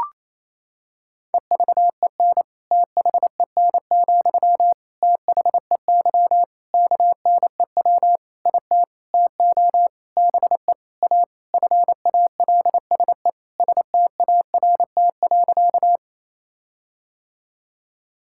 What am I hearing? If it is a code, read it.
Morse code, heard as a keyed tone, about 28 wpm, 700 Hz. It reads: EVEN THEN, THEY KNEW IT TO BE A FALSE START.
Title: Sentences from Top 1600 Words 28wpm